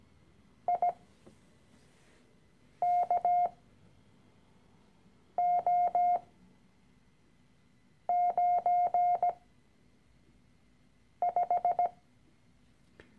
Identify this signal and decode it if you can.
Morse code, heard as a keyed tone, about 17 wpm, spaced as for 4 wpm, 710 Hz. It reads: IKO95
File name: CW-Online-Kurs / Anfänger / Lektion 013